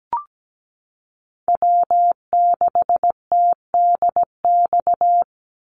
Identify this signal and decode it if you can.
Morse code, heard as a keyed tone, about 17 wpm, 700 Hz. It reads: W6TDX
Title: NA SOTA Chaser Callsigns 17wpm